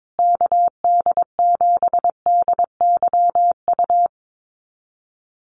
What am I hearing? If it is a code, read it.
Morse code, heard as a keyed tone, about 22 wpm, 700 Hz. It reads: KD7DYU
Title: USA Call Signs Repeated 3x 22wpm